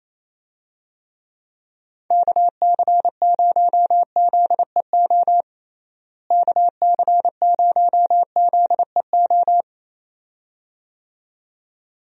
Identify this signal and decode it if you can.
Morse code, heard as a keyed tone, about 28 wpm, 700 Hz. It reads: KC0ZEO KC0ZEO
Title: USA Call Signs - 6 Characters - Repeated 2x - 28wpm